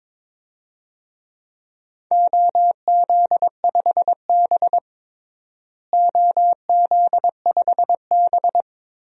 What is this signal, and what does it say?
Morse code, heard as a keyed tone, about 21 wpm, 700 Hz. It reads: OZ5B OZ5B
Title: DXCC-100 Call Signs - 4 Characters - Repeated 3x - 22wpm